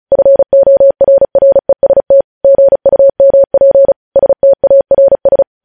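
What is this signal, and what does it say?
Morse code, heard as a keyed tone, about 36 wpm, 550 Hz. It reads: FORREST GUMP STARS
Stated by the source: Morse news headlines practice at 35 words per minute.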